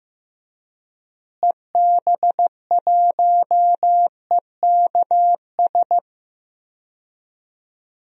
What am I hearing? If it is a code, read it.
Morse code, heard as a keyed tone, about 15 wpm, 700 Hz. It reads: EB1EKS
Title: DXCC-100 Call Signs - 6 Characters - Repeated 2x - 15wpm